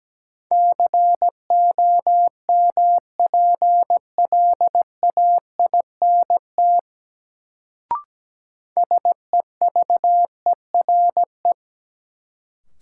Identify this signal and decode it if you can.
Morse code, heard as a keyed tone, about 17 wpm, 700 Hz. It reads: COMPLAINT SEVERE
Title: Sentences from Top 1400 Words - New Word Drill Alt - 17wpm